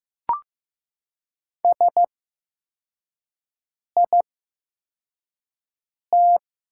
Morse code, heard as a keyed tone, about 15 wpm, spaced as for 4 wpm, 700 Hz. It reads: SIT
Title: 8x Character Spacing - Top 500 Words - 15wpm